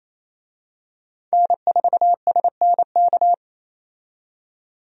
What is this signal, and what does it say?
Morse code, heard as a keyed tone, about 28 wpm, 700 Hz. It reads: N4SNK
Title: USA Call Signs - 5 Characters - 28wpm